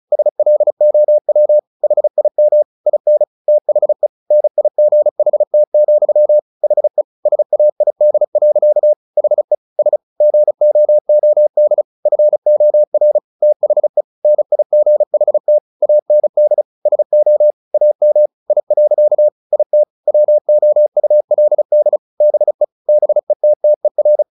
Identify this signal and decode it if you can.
Morse code, heard as a keyed tone, about 34 wpm, 600 Hz. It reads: SLOW HIM IN THE NIGHT, HE SAID. HE S GOOD FOR THE NIGHT AND SO AM I. IT WOULD BE BETTER
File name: the old man and the sea 2 35wpm